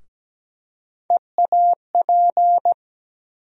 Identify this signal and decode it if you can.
Morse code, heard as a keyed tone, about 17 wpm, 700 Hz. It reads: EAP
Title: USA Call Sign Postfixes - 3 Letters 17wpm